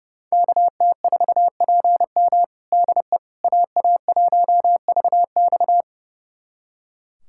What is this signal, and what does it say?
Morse code, heard as a keyed tone, about 30 wpm, 700 Hz. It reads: KT4PM DE AA1VX
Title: USA Callsign DE Callsign - 5 Characters - 30wpm